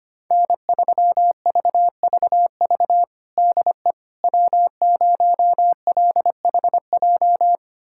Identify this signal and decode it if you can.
Morse code, heard as a keyed tone, about 25 wpm, 700 Hz. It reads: N3VVV DE W0LHJ
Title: USA Callsign DE Callsign - 5 Characters - 25wpm